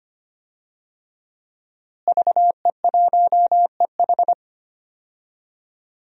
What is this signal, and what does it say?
Morse code, heard as a keyed tone, about 25 wpm, 700 Hz. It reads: VE1EH
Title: DXCC-100 Call Signs - 5 Characters - Repeated 2x - 25wpm